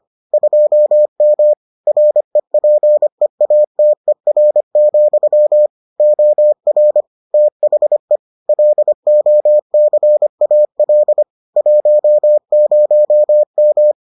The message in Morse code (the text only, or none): 2M REPEATER, OR THE LOCAL 10M